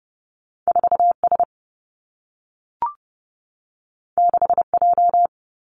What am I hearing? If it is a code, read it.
Morse code, heard as a keyed tone, about 30 wpm, 700 Hz. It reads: 4 S 6 J
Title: Two Challenging Letter-Number Combinations 30wpm